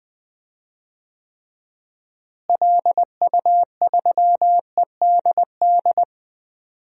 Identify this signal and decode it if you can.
Morse code, heard as a keyed tone, about 20 wpm, 700 Hz. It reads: LU3EDD
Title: DXCC-100 Call Signs - 6 Characters - Repeated 2x - 20wpm